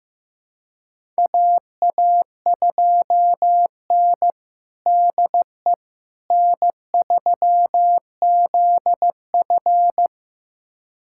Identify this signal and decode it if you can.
Morse code, heard as a keyed tone, about 15 wpm, 700 Hz. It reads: AA2N DE N3ZF
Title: USA Callsign DE Callsign - 4 Characters - 15wpm